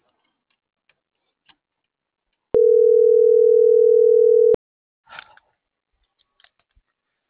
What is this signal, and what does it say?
Ringback tone (North America), 440 + 480 Hz.